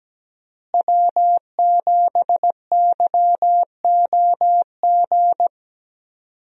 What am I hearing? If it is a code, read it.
Morse code, heard as a keyed tone, about 17 wpm, 700 Hz. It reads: W7YOG
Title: USA Call Signs - 5 Characters - Repeated 3x - 17wpm